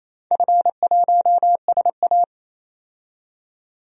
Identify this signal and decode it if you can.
Morse code, heard as a keyed tone, about 28 wpm, 700 Hz. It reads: F1SA
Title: DXCC-100 Call Signs - 4 Characters - 28wpm